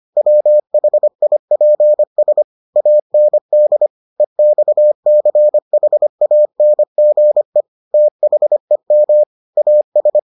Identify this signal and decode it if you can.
Morse code, heard as a keyed tone, about 25 wpm, 600 Hz. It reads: WHIPS AND EXCHANGE THEM AS